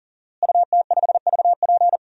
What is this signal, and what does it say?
Morse code, heard as a keyed tone, about 40 wpm, 700 Hz. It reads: UT5VP
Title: DXCC-100 Call Signs - 5 Characters - Repeated 3x - 40wpm